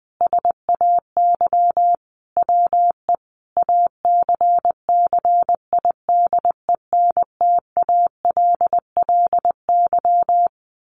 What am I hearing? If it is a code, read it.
Morse code, heard as a keyed tone, about 20 wpm, 700 Hz. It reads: SAY WE ACCIDENTALLY
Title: A Sound Of Thunder 20wpm